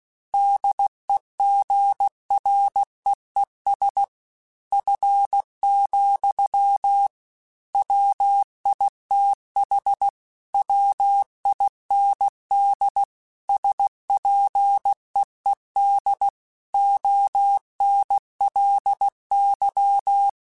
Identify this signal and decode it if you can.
Morse code, heard as a keyed tone, about 16 wpm, 800 Hz. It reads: DEGREES F, WITH WIND SPEED ONLY
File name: Weather Sets / Weather Sets-mp3s / Weather 4 28 @15